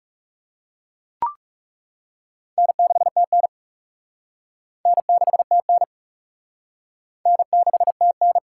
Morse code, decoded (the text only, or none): N6TN N6TN N6TN